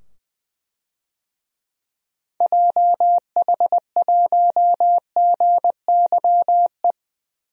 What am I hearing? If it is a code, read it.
Morse code, heard as a keyed tone, about 20 wpm, 700 Hz. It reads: JH1GYE
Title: DXCC-100 Call Signs - 6 Characters - 20wpm